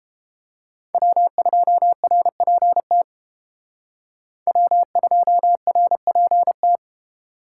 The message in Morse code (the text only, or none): W2RPT W2RPT